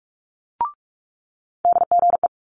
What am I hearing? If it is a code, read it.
Morse code, heard as a keyed tone, about 46 wpm, 700 Hz. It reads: DGE
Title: Multi-letter Phonemes Quick Fire 45wpm